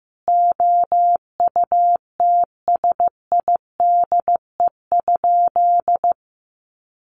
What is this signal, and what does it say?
Morse code, heard as a keyed tone, about 15 wpm, 700 Hz. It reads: OUTSIDE?
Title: Sentences from Top 1200 Words 15wpm